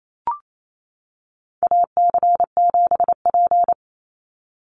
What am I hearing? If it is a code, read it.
Morse code, heard as a keyed tone, about 28 wpm, 700 Hz. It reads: AC7P
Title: NA SOTA Chaser Callsigns 28wpm